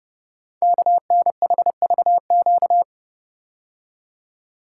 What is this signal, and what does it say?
Morse code, heard as a keyed tone, about 30 wpm, 690 Hz. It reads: KNHVQ
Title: Code Groups - 5 Letters - 30wpm